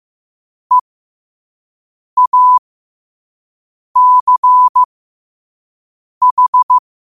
Morse code, heard as a keyed tone, about 15 wpm, spaced as for 5 wpm, 1000 Hz. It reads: EACH
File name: 2026-02-15-5wpm